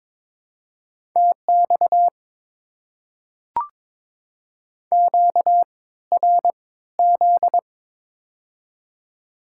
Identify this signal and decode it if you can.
Morse code, heard as a keyed tone, about 22 wpm, spaced as for 15 wpm, 690 Hz. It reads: TX QRZ
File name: Top 90 QSO Elements - 3x Character Spacing - 22wpm